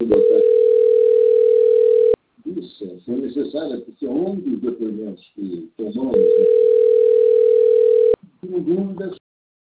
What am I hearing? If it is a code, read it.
Ringback tone (North America), 440 + 480 Hz, with speech.
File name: out- - - - - .30855